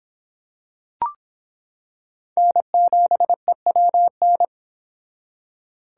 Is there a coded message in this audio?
Yes: Morse code from 2.4 to 4.5 s, heard as a keyed tone, about 26 wpm, 700 Hz. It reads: N7EWN